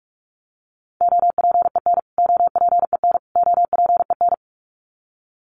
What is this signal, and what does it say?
Morse code, heard as a keyed tone, about 44 wpm, 700 Hz. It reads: OPEN OPEN OPEN
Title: Top 500 Words Repeated 45wpm